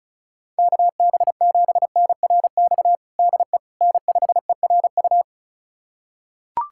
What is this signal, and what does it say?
Morse code, heard as a keyed tone, about 36 wpm, 690 Hz. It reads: KB7NRX DE N5ERU
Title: USA Callsign DE Callsign 35wpm